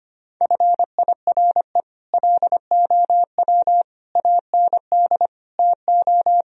Morse code, heard as a keyed tone, about 25 wpm, 700 Hz. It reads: FIRE LOW AND TO
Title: Sentences from Top 1000 Words - Review Part 1 - 25wpm